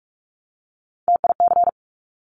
Morse code, heard as a keyed tone, about 46 wpm, 700 Hz. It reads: TIC
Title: Top Three Letter Combinations in Words 45wpm